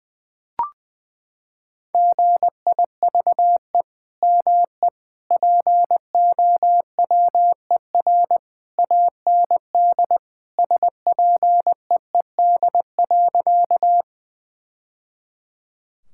Morse code, heard as a keyed tone, about 21 wpm, 700 Hz. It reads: GIVE ME POWER AND SPEED.
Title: Sentences from Top 1000 Words - Review Part 5 - 20wpm